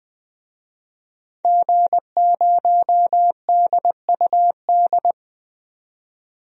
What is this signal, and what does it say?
Morse code, heard as a keyed tone, about 20 wpm, 700 Hz. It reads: G0DUD
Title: DXCC-100 Call Signs - 5 Characters - 20wpm